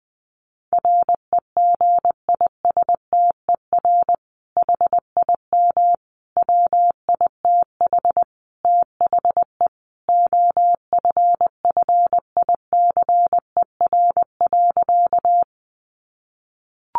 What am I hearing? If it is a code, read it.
Morse code, heard as a keyed tone, about 19 wpm, 700 Hz. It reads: REGISTER HIM WITH THE OFFICER.
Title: Sentences from Top 1200 Words 20wpm